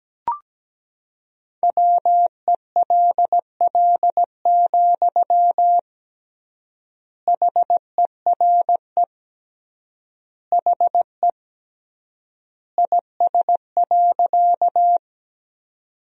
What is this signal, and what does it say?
Morse code, heard as a keyed tone, about 17 wpm, 700 Hz. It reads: WELL, HERE HE IS.
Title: 3x Word-Spacing - Sentences from Top 200 Words - 17wpm